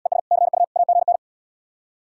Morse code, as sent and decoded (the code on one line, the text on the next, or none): . ... / -.-. ..- / .- --. -.
ES CU AGN